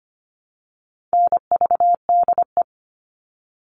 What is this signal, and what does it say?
Morse code, heard as a keyed tone, about 25 wpm, 700 Hz. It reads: NVDE
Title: All Cut Numbers - 4-Digit 25wpm